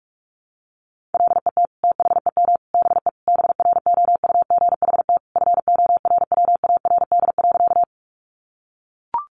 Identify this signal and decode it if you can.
Morse code, heard as a keyed tone, about 44 wpm, 700 Hz. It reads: LET THEM BE BROUGHT FORWARD.
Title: Sentences from Top 1300 Words 45wpm